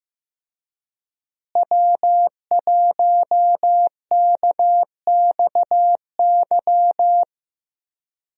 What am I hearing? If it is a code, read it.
Morse code, heard as a keyed tone, about 15 wpm, 700 Hz. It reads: W1KXY
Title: USA Call Signs - 5 Characters - Repeated 3x - 15wpm